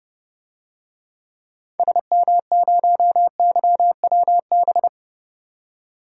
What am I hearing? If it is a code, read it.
Morse code, heard as a keyed tone, about 32 wpm, 700 Hz. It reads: SM0YWB
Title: DXCC-100 Call Signs - 6 Characters - 30wpm